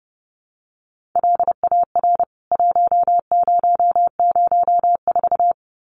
Morse code, heard as a keyed tone, about 30 wpm, 700 Hz. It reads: LAR 1004
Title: CWT Exchange 30wpm